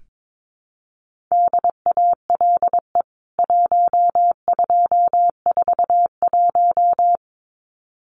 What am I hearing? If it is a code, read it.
Morse code, heard as a keyed tone, about 22 wpm, 700 Hz. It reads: DALE 1241
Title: CWT Exchange 22wpm